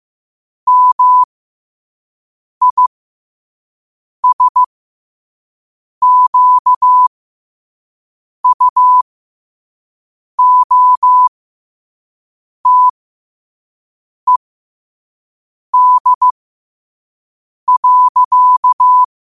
Morse code, heard as a keyed tone, about 15 wpm, spaced as for 5 wpm, 1000 Hz. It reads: MISQUOTED.
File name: qotdfiles / 5wpm